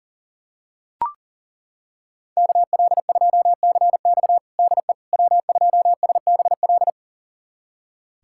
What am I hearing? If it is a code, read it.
Morse code, heard as a keyed tone, about 40 wpm, 700 Hz. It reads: KL2CX DE W2SBL
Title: USA Callsign DE Callsign - 5 Characters - 40wpm